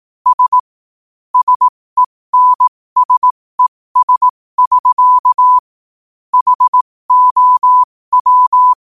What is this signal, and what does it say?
Morse code, heard as a keyed tone, about 18 wpm, spaced as for 15 wpm, 1000 Hz. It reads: S SENSES <SK> HOW
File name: qotdfiles / 15wpm